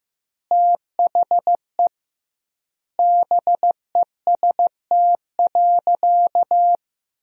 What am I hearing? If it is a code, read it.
Morse code, heard as a keyed tone, about 15 wpm, 700 Hz. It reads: THE BEST.
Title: 2x Word-Spacing - Sentences from Top 500 Words - 15wpm